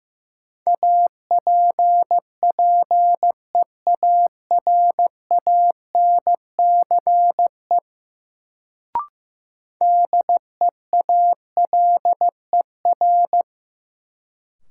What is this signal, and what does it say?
Morse code, heard as a keyed tone, about 15 wpm, 700 Hz. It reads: APPEARANCE DEALER
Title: Sentences from Top 1100 Words - New Word Drill - 15wpm